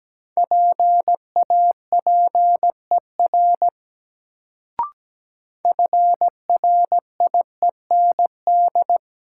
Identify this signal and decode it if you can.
Morse code, heard as a keyed tone, about 17 wpm, 700 Hz. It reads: PAPER FRIEND
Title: Sentences from Top 300 Words - New Word Drill - 17wpm